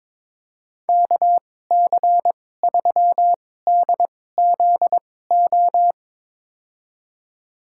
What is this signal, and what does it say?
Morse code, heard as a keyed tone, about 22 wpm, spaced as for 16 wpm, 700 Hz. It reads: KC3DZO
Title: USA Call Signs - 2x Character Spacing - 22wpm